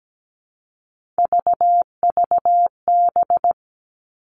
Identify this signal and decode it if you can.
Morse code, heard as a keyed tone, about 17 wpm, 700 Hz. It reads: VVB
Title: Three Challenging Letter-Number Combinations 17wpm